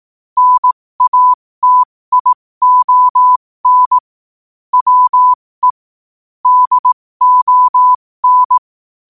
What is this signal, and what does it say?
Morse code, heard as a keyed tone, about 18 wpm, spaced as for 15 wpm, 1000 Hz. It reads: NATION WE DON
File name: qotdfiles / 15wpm